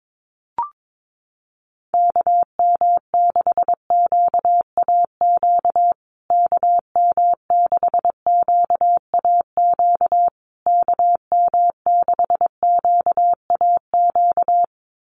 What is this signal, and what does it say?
Morse code, heard as a keyed tone, about 22 wpm, 700 Hz. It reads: KM6QAQ KM6QAQ KM6QAQ
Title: USA Call Signs Repeated 3x 22wpm